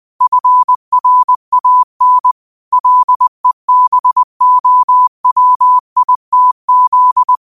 Morse code, heard as a keyed tone, about 20 wpm, 1000 Hz. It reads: FRAN LEBOWITZ
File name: qotdfiles / 20wpm